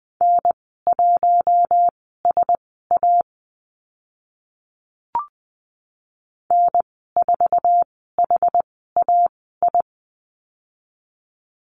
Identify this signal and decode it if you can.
Morse code, heard as a keyed tone, about 20 wpm, spaced as for 15 wpm, 700 Hz. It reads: N1SA N4HAI
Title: 062 Lesson 11g - Letter h - Call Signs 20wpm